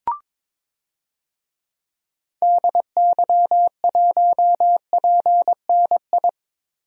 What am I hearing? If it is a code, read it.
Morse code, heard as a keyed tone, about 22 wpm, 700 Hz. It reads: DY1PNI
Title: DXCC-100 Call Signs - 6 Characters - Repeated 2x - 22wpm